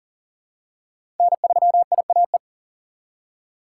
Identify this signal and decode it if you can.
Morse code, heard as a keyed tone, about 40 wpm, 700 Hz. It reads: N3IAE
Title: USA Call Signs - 5 Characters - 2x Character Spacing - 40wpm